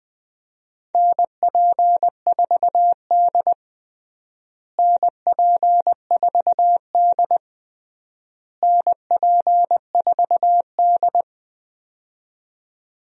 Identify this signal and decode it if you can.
Morse code, heard as a keyed tone, about 21 wpm, 700 Hz. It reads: NP4D NP4D NP4D
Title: USA Call Signs - 4 Characters - Repeated 3x - 20wpm